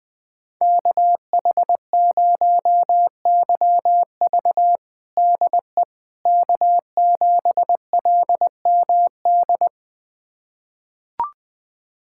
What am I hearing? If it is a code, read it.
Morse code, heard as a keyed tone, about 20 wpm, 700 Hz. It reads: KH0YV DE K7LMD
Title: USA Callsign DE Callsign - 5 Characters - 20wpm